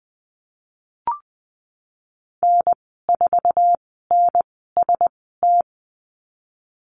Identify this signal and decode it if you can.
Morse code, heard as a keyed tone, about 19 wpm, spaced as for 14 wpm, 700 Hz. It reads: N4NST
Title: 055 Lesson 10g - Letter r - Call Signs 20wpm